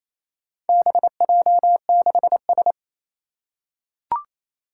Morse code, heard as a keyed tone, about 27 wpm, 700 Hz. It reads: BJ6S
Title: Four Challenging Letter-Number Combinations 28wpm